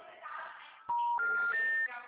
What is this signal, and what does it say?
Special information tone, with noise.
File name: out- -2009- - - .39819